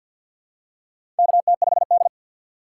Morse code, heard as a keyed tone, about 52 wpm, 700 Hz. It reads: KT5D
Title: USA Call Signs - 6x Character Spacing - 50wpm